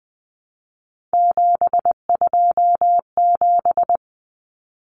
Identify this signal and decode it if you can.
Morse code, heard as a keyed tone, about 20 wpm, 700 Hz. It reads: 727
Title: Numbers - 3-Digit 20wpm